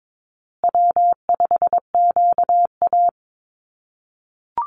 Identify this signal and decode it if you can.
Morse code, heard as a keyed tone, about 22 wpm, 700 Hz. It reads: W5QA
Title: USA Call Signs Repeated 2x 22wpm